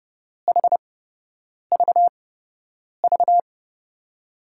Morse code, heard as a keyed tone, about 31 wpm, spaced as for 8 wpm, 700 Hz. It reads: HVV
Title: USA Call Signs - 8x Character Spacing - 30wpm